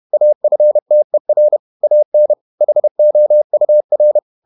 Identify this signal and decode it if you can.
Morse code, heard as a keyed tone, about 31 wpm, 600 Hz. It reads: AFTER AN HOUR
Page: Never 31wpm | CW med Gnister